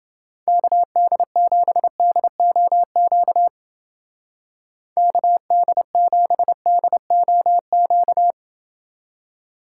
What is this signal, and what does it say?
Morse code, heard as a keyed tone, about 28 wpm, 700 Hz. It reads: KD7DOQ KD7DOQ